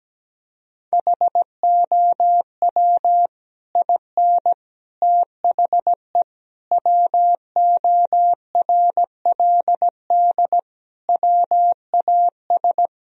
Morse code, heard as a keyed tone, about 17 wpm, 700 Hz. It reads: HOW IN THE WORLD WAS
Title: Sentences from Top 1000 Words - Review Part 4 - 17wpm